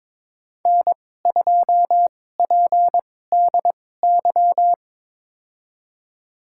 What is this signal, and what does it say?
Morse code, heard as a keyed tone, about 22 wpm, spaced as for 16 wpm, 700 Hz. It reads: N2PDY
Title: USA Call Signs - 2x Character Spacing - 22wpm